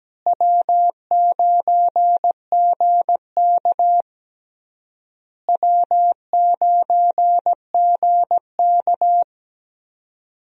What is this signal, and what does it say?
Morse code, heard as a keyed tone, about 17 wpm, 700 Hz. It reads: W9GK W9GK
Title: USA Call Signs - 4 Characters - Repeated 3x - 17wpm